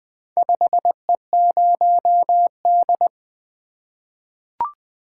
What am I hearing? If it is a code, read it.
Morse code, heard as a keyed tone, about 20 wpm, 700 Hz. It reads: 5E0D
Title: Four Letter-Number Combinations - 20wpm